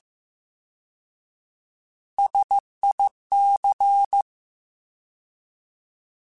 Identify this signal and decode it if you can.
Morse code, heard as a keyed tone, about 15 wpm, 800 Hz. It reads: SIC
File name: Alphabetical / Alphabetical mp3s / Ss @13